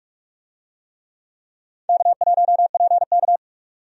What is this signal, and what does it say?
Morse code, heard as a keyed tone, about 46 wpm, 700 Hz. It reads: K1PK
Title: USA Call Signs - 4 Characters - 45wpm